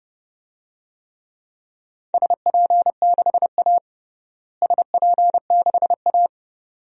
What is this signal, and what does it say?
Morse code, heard as a keyed tone, about 30 wpm, 700 Hz. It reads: SP6A SP6A
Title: DXCC-100 Call Signs - 4 Characters - Repeated 2x - 30wpm